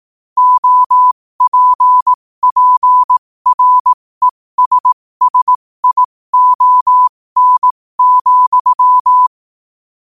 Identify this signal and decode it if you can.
Morse code, heard as a keyed tone, about 18 wpm, spaced as for 15 wpm, 1000 Hz. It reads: OPPRESSION,
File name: qotdfiles / 15wpm